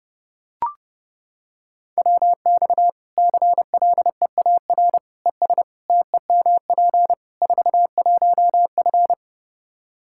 Morse code, heard as a keyed tone, about 30 wpm, 700 Hz. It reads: WX CLEAR ES TEMP 41F
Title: K1USN SST QSO Practice 30wpm